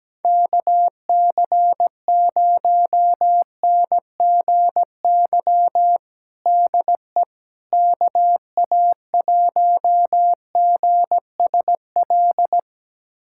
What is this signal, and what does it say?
Morse code, heard as a keyed tone, about 17 wpm, 700 Hz. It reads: KC0NGY DE KA1GSL
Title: USA Callsign DE Callsign - 6 Characters - 17wpm